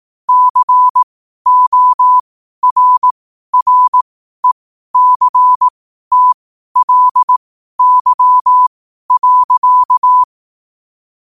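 Morse code, heard as a keyed tone, about 18 wpm, spaced as for 12 wpm, 1000 Hz. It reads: CORRECTLY.
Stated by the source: Morse code sent at 12 words per minute.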